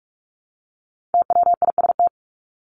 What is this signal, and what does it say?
Morse code, heard as a keyed tone, about 44 wpm, 700 Hz. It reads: TWIST
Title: Top 1751-2000 Words 45wpm